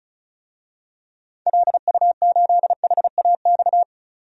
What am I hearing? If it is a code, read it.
Morse code, heard as a keyed tone, about 34 wpm, 700 Hz. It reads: LU8HAX
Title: DXCC-100 Call Signs - 6 Characters - 35wpm